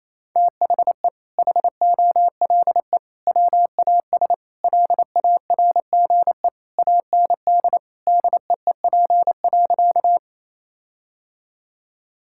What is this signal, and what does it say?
Morse code, heard as a keyed tone, about 28 wpm, 700 Hz. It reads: THE HOLE WAS LARGE AND DEEP.
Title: Sentences from Top 1600 Words 28wpm